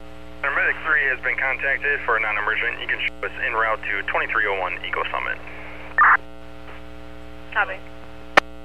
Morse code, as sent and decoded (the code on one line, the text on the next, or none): none